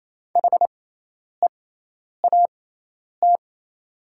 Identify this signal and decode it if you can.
Morse code, heard as a keyed tone, about 27 wpm, spaced as for 10 wpm, 690 Hz. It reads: HEAT
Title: 6x Character Spacing - Top 500 Words - 28wpm